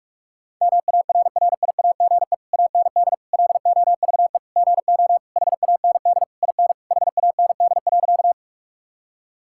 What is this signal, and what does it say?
Morse code, heard as a keyed tone, about 46 wpm, 700 Hz. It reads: MARRIAGE AND LOVE GO HAND IN HAND.
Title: Sentences from Top 1000 Words - Review Part 5 - 45wpm